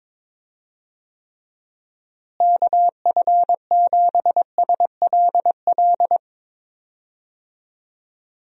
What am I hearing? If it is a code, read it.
Morse code, heard as a keyed tone, about 21 wpm, 700 Hz. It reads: KF7SLL
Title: USA Call Signs - 6 Characters - Repeated 2x - 22wpm